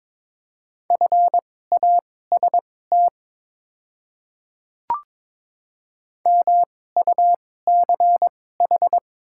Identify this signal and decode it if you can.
Morse code, heard as a keyed tone, about 22 wpm, spaced as for 16 wpm, 700 Hz. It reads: FAST MUCH
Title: 2x Character Spacing - Top 500 Words - 22wpm